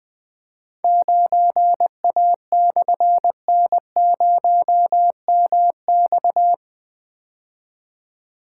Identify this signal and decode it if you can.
Morse code, heard as a keyed tone, about 20 wpm, 700 Hz. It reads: 9A/N0MX
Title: Difficult Call Signs 20wpm